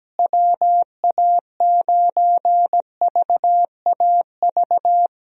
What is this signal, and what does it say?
Morse code, heard as a keyed tone, about 17 wpm, 700 Hz. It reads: WA9VAV
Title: USA Call Signs - 6 Characters - Repeated 3x - 17wpm